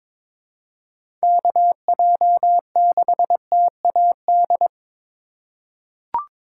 Morse code, heard as a keyed tone, about 21 wpm, 700 Hz. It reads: KJ6TAD
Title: USA Call Signs - 3x Character Spacing - 22wpm